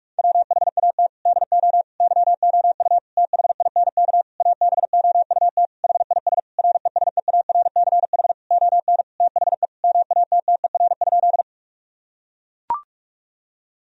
Morse code, heard as a keyed tone, about 41 wpm, 700 Hz. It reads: WHAT DO YOU THINK ABOUT HIS RESEARCH ON THE MATTER?
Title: Sentences from Top 1000 Words - Review Part 4 - 45wpm